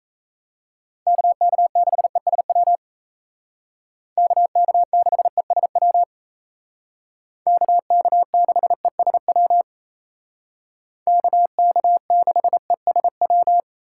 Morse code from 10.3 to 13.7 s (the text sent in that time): KK6ESW